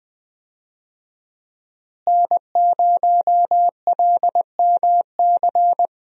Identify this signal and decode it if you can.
Morse code, heard as a keyed tone, about 20 wpm, 700 Hz. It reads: N0LMC
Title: USA Call Signs - 5 Characters - Repeated 3x - 20wpm